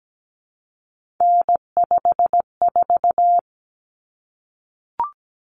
Morse code, heard as a keyed tone, about 17 wpm, 700 Hz. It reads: N54
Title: Common Cut Numbers - 3-Digit 17wpm